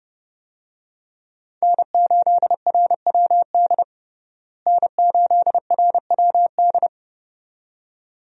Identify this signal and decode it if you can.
Morse code, heard as a keyed tone, about 30 wpm, 700 Hz. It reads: N8RWD N8RWD
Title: USA Call Signs - 5 Characters - Repeated 2x - 30wpm